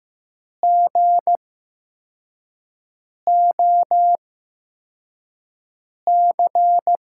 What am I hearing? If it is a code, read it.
Morse code, heard as a keyed tone, about 15 wpm, spaced as for 4 wpm, 700 Hz. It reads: GOC
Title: USA Call Signs - 5 Characters - 8x Character Spacing - 15wpm